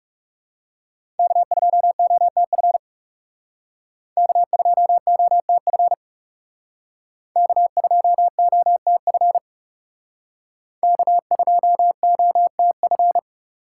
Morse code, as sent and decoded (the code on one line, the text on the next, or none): -.- ..--- --- - ..-. / -.- ..--- --- - ..-. / -.- ..--- --- - ..-. / -.- ..--- --- - ..-.
K2OTF K2OTF K2OTF K2OTF